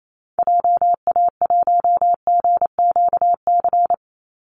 Morse code, heard as a keyed tone, about 28 wpm, 700 Hz. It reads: JA1GQC
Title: Call Signs - 28wpm